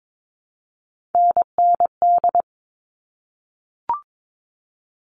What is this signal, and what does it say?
Morse code, heard as a keyed tone, about 22 wpm, 700 Hz. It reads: NND
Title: All Cut Numbers - 3-Digit 22wpm